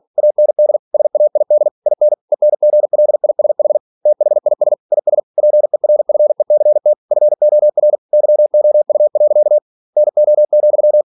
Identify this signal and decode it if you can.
Morse code, heard as a keyed tone, about 46 wpm, 600 Hz. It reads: AND SAID IN ENGLISH THIS IS PERFECT FOR YOU. NO,
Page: Never 47wpm